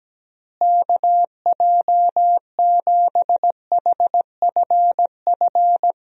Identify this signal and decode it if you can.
Morse code, heard as a keyed tone, about 17 wpm, 700 Hz. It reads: KJ7HFF
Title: USA Call Signs - 6 Characters - Repeated 3x - 17wpm